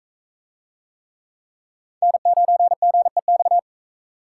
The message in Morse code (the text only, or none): N9GEX